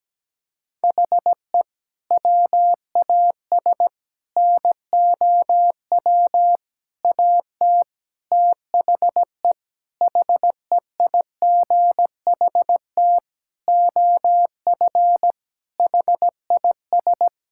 Morse code, heard as a keyed tone, about 17 wpm, 700 Hz. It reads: HE WAS NOW AT THE HEIGHT OF HIS
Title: Sentences from Top 1600 Words 17wpm